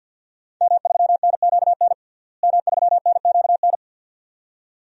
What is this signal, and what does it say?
Morse code, heard as a keyed tone, about 47 wpm, 700 Hz. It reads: M3NQN M3NQN
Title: DXCC-100 Call Signs - 5 Characters - Repeated 2x - 50wpm